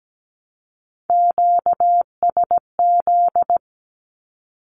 Morse code, heard as a keyed tone, about 17 wpm, 700 Hz. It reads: QSZ
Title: Q-Codes Repeated 3x 17wpm